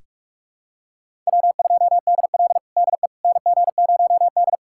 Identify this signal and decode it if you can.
Morse code, heard as a keyed tone, about 46 wpm, 700 Hz. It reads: W2DL DE NG0D
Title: USA Callsign DE Callsign - 4 Characters - 45wpm